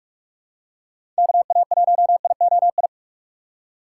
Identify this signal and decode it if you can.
Morse code, heard as a keyed tone, about 45 wpm, 700 Hz. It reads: KA1IOI
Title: USA Call Signs - 6 Characters - 6x Character Spacing - 45wpm